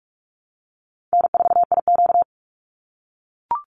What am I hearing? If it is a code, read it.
Morse code, heard as a keyed tone, about 44 wpm, 700 Hz. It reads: N4IQ
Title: Call Signs - 45wpm